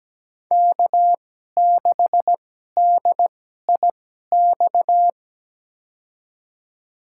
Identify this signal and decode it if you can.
Morse code, heard as a keyed tone, about 17 wpm, spaced as for 12 wpm, 700 Hz. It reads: K6DIX
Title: USA Call Signs - 5 Characters - 2x Character Spacing - 17wpm